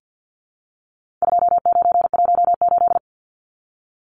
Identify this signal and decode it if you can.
Morse code, heard as a keyed tone, about 50 wpm, 700 Hz. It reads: 2918
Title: Numbers - 4-Digit 50wpm